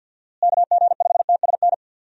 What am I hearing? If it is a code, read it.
Morse code, heard as a keyed tone, about 48 wpm, 700 Hz. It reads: KG5TSN
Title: USA Call Signs - 6 Characters - Repeated 2x - 50wpm